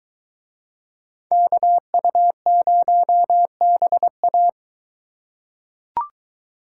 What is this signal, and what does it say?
Morse code, heard as a keyed tone, about 23 wpm, 700 Hz. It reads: KU0BA